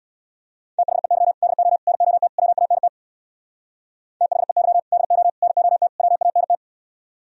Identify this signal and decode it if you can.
Morse code, heard as a keyed tone, about 60 wpm, 700 Hz. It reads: THEY DO NOT WANT THEY DO NOT WANT
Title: Speed-Racing - Sets of 4 Words 50wpm